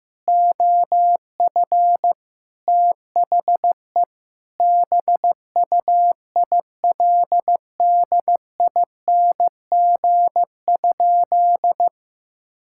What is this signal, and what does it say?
Morse code, heard as a keyed tone, about 15 wpm, 700 Hz. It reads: OF THE BUILDING?
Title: Sentences from Top 1000 Words - Review Part 5 - 15wpm